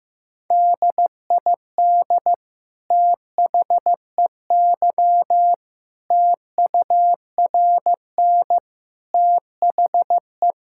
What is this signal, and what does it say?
Morse code, heard as a keyed tone, about 15 wpm, 700 Hz. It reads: DID THEY TURN THE
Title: Sentences from Top 1000 Words - Review Part 5 - 15wpm